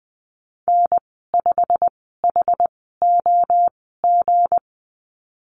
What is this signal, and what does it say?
Morse code, heard as a keyed tone, about 20 wpm, spaced as for 15 wpm, 700 Hz. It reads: N5HOG
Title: 161 Lesson 26g - Letter g - Call Signs 20wpm